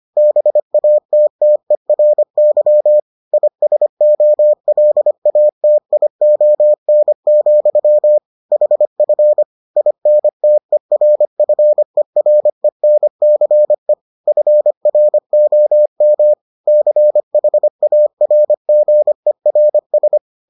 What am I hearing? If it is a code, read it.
Morse code, heard as a keyed tone, about 25 wpm, 600 Hz. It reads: BATTERY ISOLATION, HF INTERFERENCE FROM CHARGERS